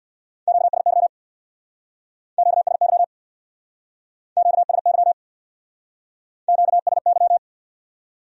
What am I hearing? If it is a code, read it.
Morse code, heard as a keyed tone, about 62 wpm, 700 Hz. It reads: QSY QSY QSY QSY
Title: Speed-Racing Q-Codes - 50wpm